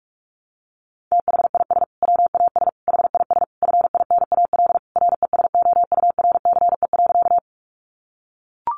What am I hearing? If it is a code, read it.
Morse code, heard as a keyed tone, about 44 wpm, 700 Hz. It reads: THIS WAS HIS FINAL RESOURCE.
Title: Sentences from Top 1300 Words 45wpm